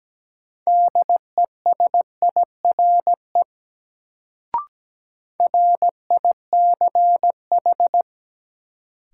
Morse code, heard as a keyed tone, about 17 wpm, 700 Hz. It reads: DESIRE RICH
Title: Sentences from Top 1100 Words - New Word Drill Alt - 17wpm